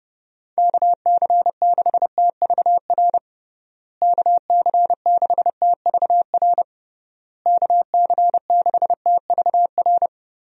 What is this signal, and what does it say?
Morse code, heard as a keyed tone, about 30 wpm, 700 Hz. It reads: KC6TVR KC6TVR KC6TVR
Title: USA Call Signs - 6 Characters - Repeated 3x - 30wpm